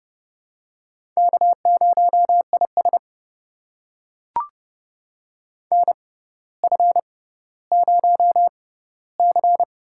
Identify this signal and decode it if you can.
Morse code, heard as a keyed tone, about 30 wpm, spaced as for 15 wpm, 700 Hz. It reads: K0IS NF0C
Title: USA Call Signs - 4 Characters - 6x Character Spacing - 30wpm